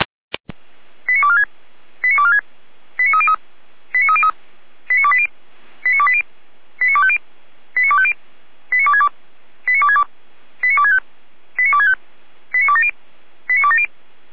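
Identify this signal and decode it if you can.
Five-tone selective call (ZVEI), heard as quick runs of five tones, tones of about 70 ms. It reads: 89246 89246 89303 89303 89280 89280 89250 89250 89262 89262 89265 89265 89270 89270
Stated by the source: ZVEI five-tone selective call 89208, received by SDR.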